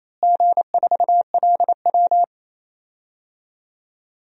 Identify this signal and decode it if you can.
Morse code, heard as a keyed tone, about 28 wpm, 700 Hz. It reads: G4LW
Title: DXCC-100 Call Signs - 4 Characters - Repeated 2x - 28wpm